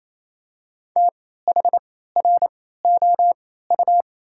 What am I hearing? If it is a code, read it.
Morse code, heard as a keyed tone, about 28 wpm, spaced as for 16 wpm, 700 Hz. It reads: THROU
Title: 3x Character Spacing - Top 200 Words - 28wpm